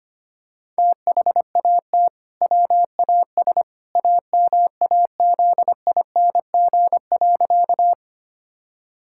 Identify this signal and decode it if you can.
Morse code, heard as a keyed tone, about 25 wpm, 700 Hz. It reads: THAT WAS AMAZING.
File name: practice / 25wpm / Sentences with 2-Word Phrases - 25wpm